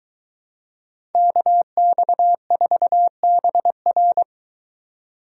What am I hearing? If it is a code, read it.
Morse code, heard as a keyed tone, about 23 wpm, 700 Hz. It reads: KX4BR